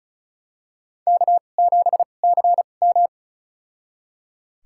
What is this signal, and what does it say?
Morse code, heard as a keyed tone, about 36 wpm, spaced as for 25 wpm, 690 Hz. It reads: K7CM
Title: USA Call Signs - 4 Characters - 2x Character Spacing - 35wpm